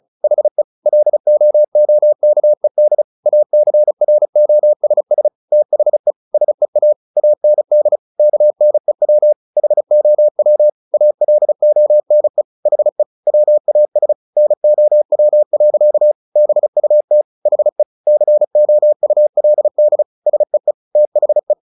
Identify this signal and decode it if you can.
Morse code, heard as a keyed tone, about 34 wpm, 600 Hz. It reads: HE LOOKED ACROSS THE SEA AND KNEW HOW ALONE HE WAS NOW. BUT HE COULD SEE THE
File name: the old man and the sea / the old man and the sea 1 35wpm